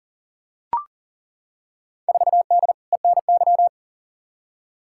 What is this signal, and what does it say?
Morse code, heard as a keyed tone, about 40 wpm, 700 Hz. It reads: 4D ENY
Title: Field Day Exchanges Encore - 40wpm